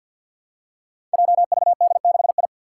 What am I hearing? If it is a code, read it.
Morse code, heard as a keyed tone, about 49 wpm, 700 Hz. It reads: JVD6I
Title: Five Challenging Letter-Number Combinations 50wpm